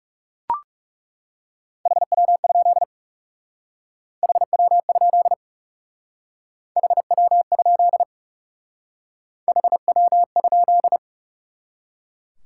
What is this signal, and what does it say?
Morse code, heard as a keyed tone, about 35 wpm, 700 Hz. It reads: HW? HW? HW? HW?
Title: Speed-Racing Top 90 QSO Elements 30wpm